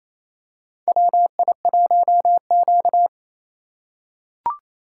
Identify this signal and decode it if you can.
Morse code, heard as a keyed tone, about 28 wpm, 700 Hz. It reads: WI1Q
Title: USA Call Signs - 4 Characters - 3x Character Spacing - 28wpm